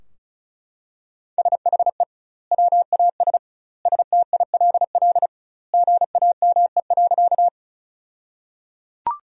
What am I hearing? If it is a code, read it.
Morse code, heard as a keyed tone, about 34 wpm, 700 Hz. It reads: SHE WAS STILL GAME.
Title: 2x Word-Spacing - Sentences from Top 500 Words - 35wpm